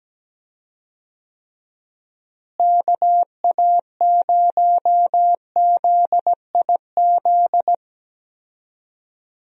Morse code, heard as a keyed tone, about 17 wpm, 700 Hz. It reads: KA0ZIZ
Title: USA Call Signs - 6 Characters - Repeated 3x - 17wpm